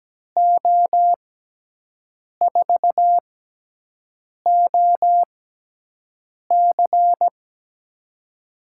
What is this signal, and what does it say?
Morse code, heard as a keyed tone, about 17 wpm, spaced as for 6 wpm, 700 Hz. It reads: O4OC
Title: USA Call Signs - 5 Characters - 6x Character Spacing - 17wpm